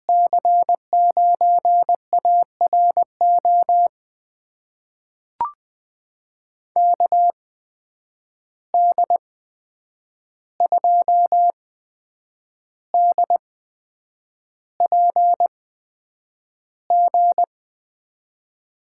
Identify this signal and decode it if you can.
Morse code, heard as a keyed tone, about 21 wpm, spaced as for 8 wpm, 700 Hz. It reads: C9ARO KD2DPG
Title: USA Call Signs - 6 Characters - 8x Character Spacing - 20wpm